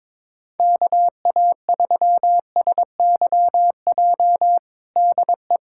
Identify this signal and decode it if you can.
Morse code, heard as a keyed tone, about 22 wpm, 700 Hz. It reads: KA3SYJ DE
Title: USA Callsign DE Callsign 22wpm